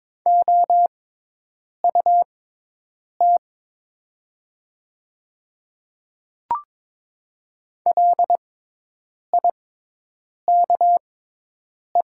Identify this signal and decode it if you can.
Morse code, heard as a keyed tone, about 23 wpm, spaced as for 8 wpm, 700 Hz. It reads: OUT LIKE
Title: 6x Character Spacing - Top 300 Words - 22wpm